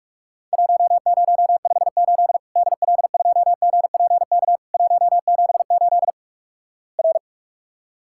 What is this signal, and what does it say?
Morse code, heard as a keyed tone, about 44 wpm, 700 Hz. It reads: 1058 DL2GPK 178 R
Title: Worked All Europe DX Contest - QTC Exchange - 45wpm